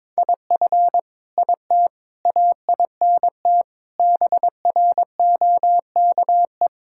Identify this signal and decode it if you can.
Morse code, heard as a keyed tone, about 22 wpm, 700 Hz. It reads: IF IT AINT BROKE
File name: practice / 22wpm / Sets of 7 Words - 22wpm